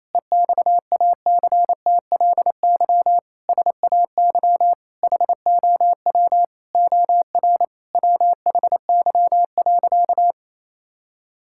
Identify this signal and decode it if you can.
Morse code, heard as a keyed tone, about 28 wpm, 700 Hz. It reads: EXACTLY SAY HOW OR WHY.
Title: Sentences from Top 1000 Words - Review Part 6 - 28wpm